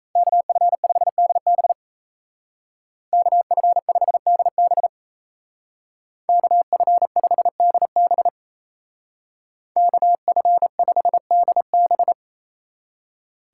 Morse code, heard as a keyed tone, about 33 wpm, 700 Hz. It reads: KF5DB KF5DB KF5DB KF5DB
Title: Speed Racing - USA 2x2 Call Signs 28wpm